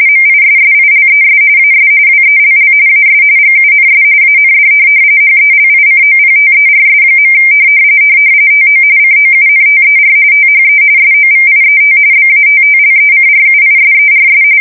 RTTY, about 45.45 baud. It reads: MISSILES AS WELL AS GUNS AND HELICOPTERS. TO COUNTER THE SOVIET BATTLECRUISERS
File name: fsk170 / USS-New-Jersey